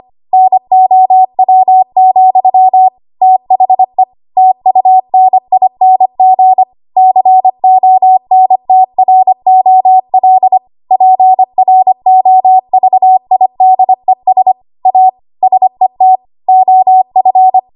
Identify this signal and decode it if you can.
Morse code, heard as a keyed tone, about 25 wpm, 750 Hz. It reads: NOW, THE TUNING CONTROL PROVIDES A SET OF
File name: Archive / 25 WPM / 25WPM